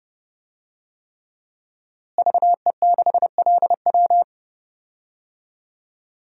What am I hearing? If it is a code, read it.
Morse code, heard as a keyed tone, about 30 wpm, 700 Hz. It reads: VE6LW
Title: DXCC-100 Call Signs - 5 Characters - Repeated 3x - 30wpm